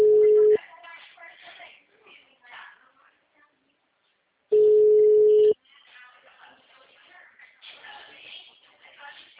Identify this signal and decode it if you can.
Ringback tone (Europe, 425 Hz), with speech.
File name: out- -2012- - - .49828